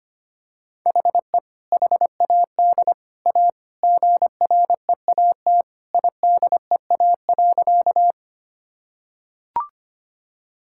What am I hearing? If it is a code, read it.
Morse code, heard as a keyed tone, about 25 wpm, 700 Hz. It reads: HE HAD A GREAT IDEA.
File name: practice / 25wpm / Sentences with 3-Word Phrases - 25wpm